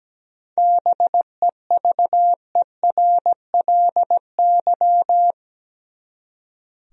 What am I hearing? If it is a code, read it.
Morse code, heard as a keyed tone, about 17 wpm, 700 Hz. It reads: BEVERLY
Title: USA Common Names - 17wpm